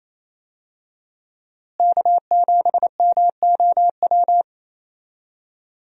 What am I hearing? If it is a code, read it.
Morse code, heard as a keyed tone, about 28 wpm, 700 Hz. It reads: K7MOW
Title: USA Call Signs - 5 Characters - Repeated 2x - 28wpm